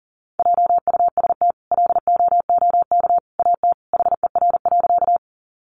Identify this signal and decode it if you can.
Morse code, heard as a keyed tone, about 40 wpm, 700 Hz. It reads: JUST LOOK AT HER.
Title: Sentences from Top 200 Words 40wpm